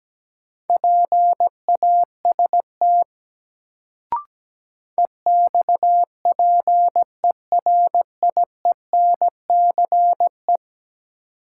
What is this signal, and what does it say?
Morse code, heard as a keyed tone, about 17 wpm, 700 Hz. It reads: PAST EXPERIENCE
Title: Sentences from Top 600 Words - New Word Drill - 17wpm